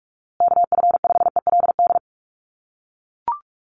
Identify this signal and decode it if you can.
Morse code, heard as a keyed tone, about 44 wpm, 700 Hz. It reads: KF5ELD
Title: USA Call Signs Repeated 2x 45wpm